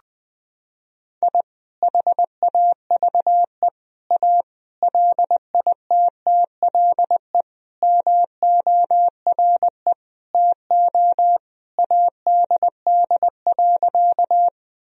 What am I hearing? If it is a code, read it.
Morse code, heard as a keyed tone, about 20 wpm, 700 Hz. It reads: I HAVE A LITTLE MORE TO ADD.
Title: Sentences from Top 1000 Words - Review Part 1 - 20wpm